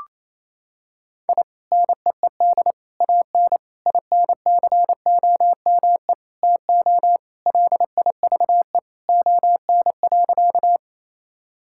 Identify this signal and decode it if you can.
Morse code, heard as a keyed tone, about 28 wpm, 700 Hz. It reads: I NEED AN INCOME TO LIVE ON.
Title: Sentences from Top 1000 Words - Review Part 5 - 28wpm